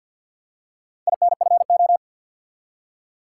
Morse code, heard as a keyed tone, about 51 wpm, 700 Hz. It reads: INFO
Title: Top 180 QSO Elements 50wpm